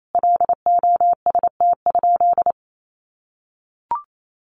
Morse code, heard as a keyed tone, about 28 wpm, 700 Hz. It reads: LOST?
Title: Sentences from Top 1400 Words 28wpm